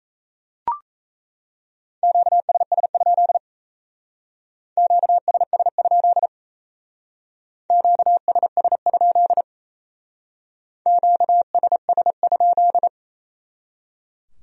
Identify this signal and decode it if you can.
Morse code, heard as a keyed tone, about 34 wpm, 700 Hz. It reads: QSS? QSS? QSS? QSS?
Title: Speed-Racing Q-Codes - 28wpm